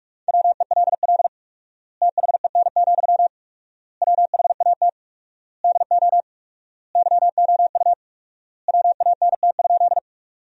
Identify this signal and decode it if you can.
Morse code, heard as a keyed tone, about 46 wpm, 700 Hz. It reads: WELL THEN, WHAT DO YOU WANT?
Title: 4x Word-Spacing - Sentences from Top 200 Words - 45wpm